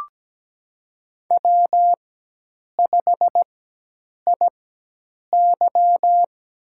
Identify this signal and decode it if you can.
Morse code, heard as a keyed tone, about 17 wpm, spaced as for 8 wpm, 700 Hz. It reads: W5IY
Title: USA Call Signs - 4x Character Spacing - 17wpm